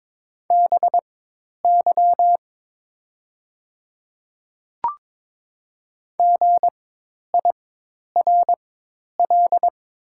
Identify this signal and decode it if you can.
Morse code, heard as a keyed tone, about 22 wpm, spaced as for 10 wpm, 700 Hz. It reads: BY GIRL
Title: 4x Character Spacing - Top 500 Words - 22wpm